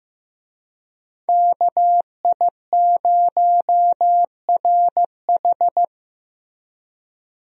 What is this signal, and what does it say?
Morse code, heard as a keyed tone, about 15 wpm, 700 Hz. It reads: KI0RH
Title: USA Call Signs - 5 Characters - Repeated 2x - 15wpm